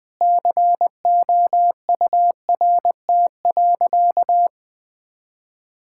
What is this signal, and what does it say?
Morse code, heard as a keyed tone, about 21 wpm, 700 Hz. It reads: COURT.
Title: Sentences from Top 1600 Words 20wpm